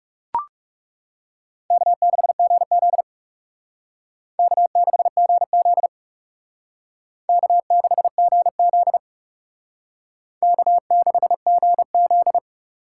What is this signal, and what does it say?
Morse code, heard as a keyed tone, about 38 wpm, 700 Hz. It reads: K6GZ K6GZ K6GZ K6GZ